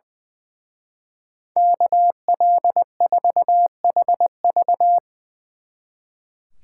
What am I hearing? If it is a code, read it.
Morse code, heard as a keyed tone, about 23 wpm, 700 Hz. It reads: KL4HV
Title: Speed Racing - USA 2x2 Call Signs 20wpm